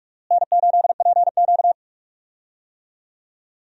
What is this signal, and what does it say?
Morse code, heard as a keyed tone, about 44 wpm, 700 Hz. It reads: N8PQ
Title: USA Call Signs - 4 Characters - Repeated 2x - 45wpm